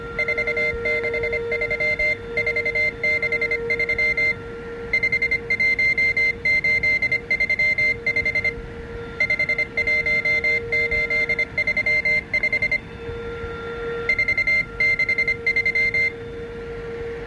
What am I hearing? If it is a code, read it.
Morse code, heard as a keyed tone, about 25 wpm, 2100 Hz. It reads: 463 463 51835 51835 463